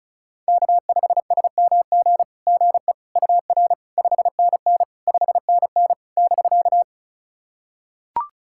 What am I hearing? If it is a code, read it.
Morse code, heard as a keyed tone, about 36 wpm, 700 Hz. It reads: K5SMG GE UR 5NN 5NN <BK>
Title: SOTA QSO Practice 35wpm